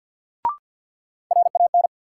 Morse code, heard as a keyed tone, about 48 wpm, 700 Hz. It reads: RAN
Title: Sentences from Top 500 Words - New Word Drill - 50wpm